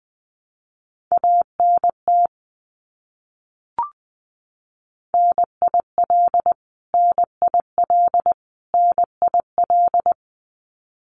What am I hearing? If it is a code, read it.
Morse code, heard as a keyed tone, about 19 wpm, 700 Hz. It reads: ANT NIL NIL NIL
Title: Abbreviations Repeated 3x - 20wpm